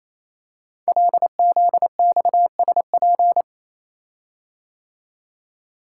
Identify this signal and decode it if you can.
Morse code, heard as a keyed tone, about 27 wpm, 700 Hz. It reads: LZXSP
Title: Code Groups - 5 Letters - 28wpm